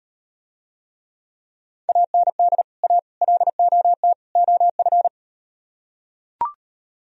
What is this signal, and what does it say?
Morse code, heard as a keyed tone, about 38 wpm, 700 Hz. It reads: AND A LOT OF